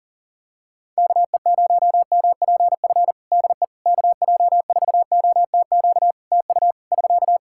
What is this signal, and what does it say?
Morse code, heard as a keyed tone, about 40 wpm, 700 Hz. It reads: KE0MPF DE KJ4OTQ TU <SK>
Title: Practice QSO - Exchange 4 - Ending - 40wpm